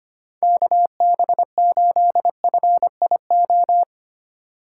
Morse code, heard as a keyed tone, about 25 wpm, 700 Hz. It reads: KB8FIO
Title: USA Call Signs - 6 Characters - 4x Character Spacing - 25wpm